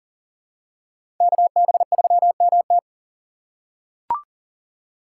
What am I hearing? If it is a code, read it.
Morse code, heard as a keyed tone, about 40 wpm, 700 Hz. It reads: KB3MT
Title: USA Call Signs - 5 Characters - 4x Character Spacing - 40wpm